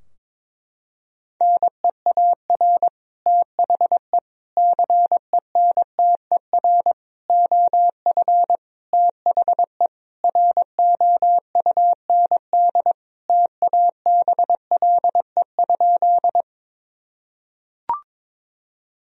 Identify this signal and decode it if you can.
Morse code, heard as a keyed tone, about 22 wpm, 700 Hz. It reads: NEAR THE CENTER OF THE ROUND TABLE?
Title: Sentences from Top 1000 Words - Review Part 2 - 22wpm